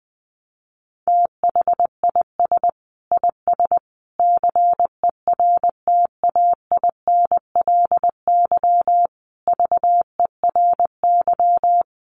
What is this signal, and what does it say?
Morse code, heard as a keyed tone, about 20 wpm, 700 Hz. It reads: THIS IS CERTAINLY VERY
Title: Sentences from Top 1300 Words 20wpm